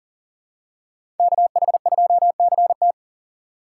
Morse code, heard as a keyed tone, about 40 wpm, 700 Hz. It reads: KH2CT
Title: USA Call Signs - 5 Characters - 2x Character Spacing - 40wpm